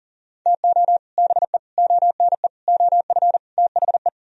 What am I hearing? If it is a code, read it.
Morse code, heard as a keyed tone, about 51 wpm, 700 Hz. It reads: TO BE ONE OF THE